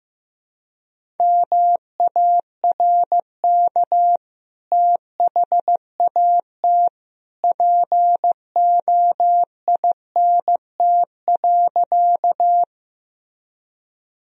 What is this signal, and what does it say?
Morse code, heard as a keyed tone, about 15 wpm, 700 Hz. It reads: MARK THAT POINT.
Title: Sentences from Top 1000 Words - Review Part 1 - 15wpm